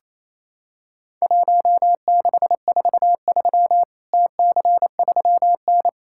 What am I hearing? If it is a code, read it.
Morse code, heard as a keyed tone, about 27 wpm, 700 Hz. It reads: 1643 TC3N
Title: Worked All Europe DX Contest - QTC Exchange - 28wpm